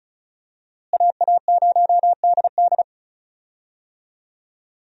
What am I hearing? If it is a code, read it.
Morse code, heard as a keyed tone, about 36 wpm, 700 Hz. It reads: AA0DD
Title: USA Call Signs - 5 Characters - Repeated 3x - 35wpm